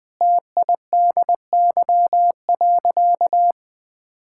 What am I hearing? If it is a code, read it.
Morse code, heard as a keyed tone, about 20 wpm, 700 Hz. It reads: TIDY.
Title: Easy Sentences with Hints - 20wpm